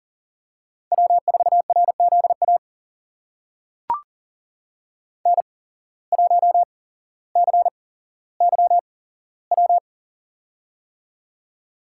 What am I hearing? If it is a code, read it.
Morse code, heard as a keyed tone, about 40 wpm, spaced as for 17 wpm, 700 Hz. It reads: W4RZA N1CYW
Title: USA Call Signs - 5 Characters - 8x Character Spacing - 40wpm